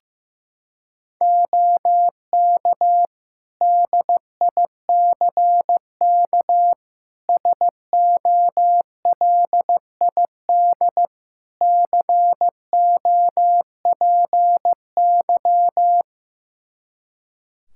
Morse code, heard as a keyed tone, about 15 wpm, 700 Hz. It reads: OK DICK SOLID COPY
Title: Practice QSO - Exchange 3 - More Info - 15wpm